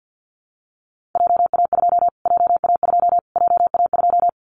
Morse code, heard as a keyed tone, about 48 wpm, 700 Hz. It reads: JA2 JA2 JA2
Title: Callsign Prefixes for Top 25 DXCC - Repeated 3x - 50wpm